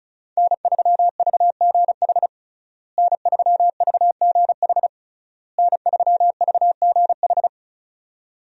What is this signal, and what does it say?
Morse code, heard as a keyed tone, about 34 wpm, 700 Hz. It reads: N3VGH N3VGH N3VGH
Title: USA Call Signs - 5 Characters - Repeated 3x - 35wpm